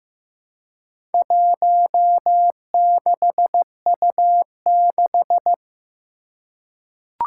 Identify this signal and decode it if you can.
Morse code, heard as a keyed tone, about 15 wpm, 700 Hz. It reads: 16U6
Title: Four Challenging Letter-Number Combinations 15wpm